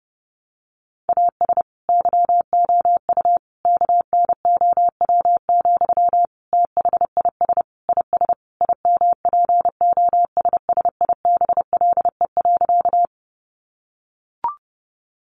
Morse code, heard as a keyed tone, about 30 wpm, 700 Hz. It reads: AS YOU KNOW, THIS IS IMPOSSIBLE.
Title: Sentences from Top 1100 Words 30wpm